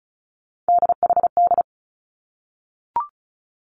Morse code, heard as a keyed tone, about 36 wpm, 700 Hz. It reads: DHD
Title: Three Challenging Letter-Number Combinations 35wpm